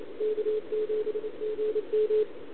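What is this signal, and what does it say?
Congestion tone (425 Hz).
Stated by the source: Contest CW from OK1RI on 80 m during the OKOM DX contest, recorded by a QRP station using a KX3 and a wire antenna.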